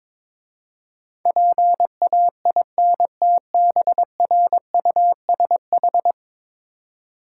Morse code, heard as a keyed tone, about 22 wpm, 700 Hz. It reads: PAINTBRUSH
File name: Difficult Words with Hints - 22wpm